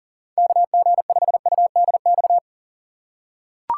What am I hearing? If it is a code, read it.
Morse code, heard as a keyed tone, about 40 wpm, 700 Hz. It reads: KG5UDX
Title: USA Call Signs - 6 Characters - 4x Character Spacing - 40wpm